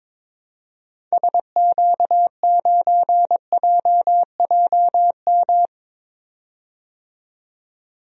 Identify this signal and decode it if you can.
Morse code, heard as a keyed tone, about 22 wpm, 700 Hz. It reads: SQ9JJM
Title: DXCC-100 Call Signs - 6 Characters - 22wpm